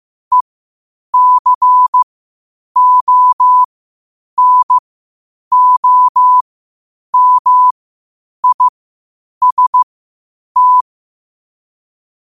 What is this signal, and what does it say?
Morse code, heard as a keyed tone, about 15 wpm, spaced as for 8 wpm, 1000 Hz. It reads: ECONOMIST
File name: qotdfiles / 8wpm